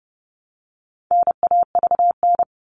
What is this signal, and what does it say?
Morse code, heard as a keyed tone, about 30 wpm, 700 Hz. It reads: NAVN
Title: All Cut Numbers - 4-Digit 30wpm